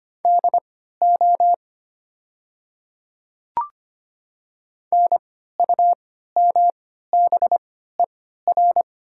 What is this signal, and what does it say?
Morse code, heard as a keyed tone, about 25 wpm, spaced as for 14 wpm, 700 Hz. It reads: DO NUMBER
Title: 3x Character Spacing - Top 100 Words - 25wpm